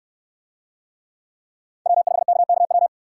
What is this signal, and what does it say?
Morse code, heard as a keyed tone, about 67 wpm, 700 Hz. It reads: W5KKW